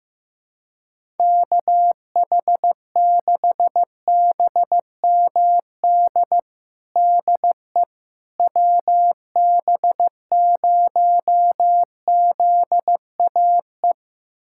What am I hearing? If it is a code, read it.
Morse code, heard as a keyed tone, about 15 wpm, 700 Hz. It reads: KH6BMD DE WB0ZAE
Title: USA Callsign DE Callsign - 6 Characters - 15wpm